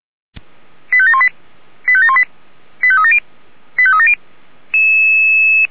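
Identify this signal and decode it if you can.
Five-tone selective call (ZVEI), heard as quick runs of five tones, tones of about 70 ms. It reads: 85718 85718 85370 85370
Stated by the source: ZVEI five-tone call 89459 received by SDR, channel 464.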